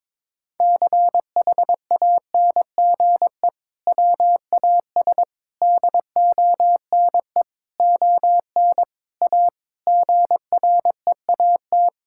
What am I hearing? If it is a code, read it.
Morse code, heard as a keyed tone, about 21 wpm, 700 Hz. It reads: CHANGE WAS DONE ON A GREAT
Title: Sentences from Top 1000 Words - Review Part 5 - 22wpm